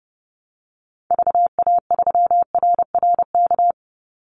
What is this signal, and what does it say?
Morse code, heard as a keyed tone, about 30 wpm, 700 Hz. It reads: VA3RRK
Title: Call Signs - 30wpm